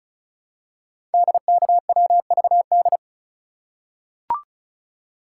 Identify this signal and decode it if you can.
Morse code, heard as a keyed tone, about 36 wpm, 690 Hz. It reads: DKWVD
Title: Code Groups - 5 Letters - 35wpm